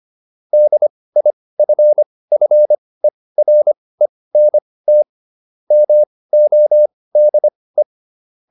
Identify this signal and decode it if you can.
Morse code, heard as a keyed tone, about 25 wpm, spaced as for 18 wpm, 600 Hz. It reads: DIFFERENT MODE